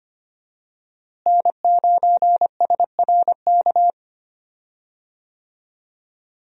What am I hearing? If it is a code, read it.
Morse code, heard as a keyed tone, about 25 wpm, 700 Hz. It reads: N9SRK
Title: USA Call Signs - 5 Characters - Repeated 2x - 25wpm